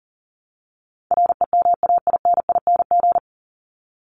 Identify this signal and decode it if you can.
Morse code, heard as a keyed tone, about 40 wpm, 700 Hz. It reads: REMAINING
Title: ING Suffix - Top 100 Words with Base Words 40wpm